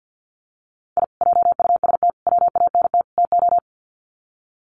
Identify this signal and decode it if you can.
Morse code, heard as a keyed tone, about 48 wpm, 700 Hz. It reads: I JUST WANT TO
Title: Sets of 4 Words 50wpm